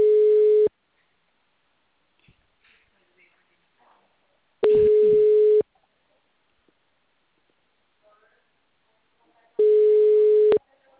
Ringback tone (Europe, 425 Hz).
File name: out- -2004- - - .21018